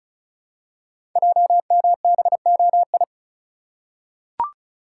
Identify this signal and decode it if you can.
Morse code, heard as a keyed tone, about 36 wpm, 700 Hz. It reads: JMBOI
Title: Code Groups - 5 Letters - 35wpm